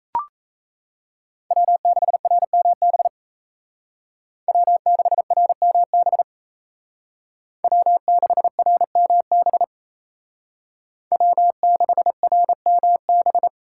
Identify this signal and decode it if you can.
Morse code, heard as a keyed tone, about 37 wpm, 700 Hz. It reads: W6RMB W6RMB W6RMB W6RMB